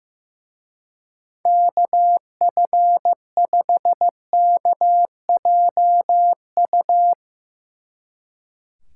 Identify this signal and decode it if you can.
Morse code, heard as a keyed tone, about 16 wpm, 700 Hz. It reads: KF5KJU
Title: Speed Racing - USA 2x3 Call Signs 15wpm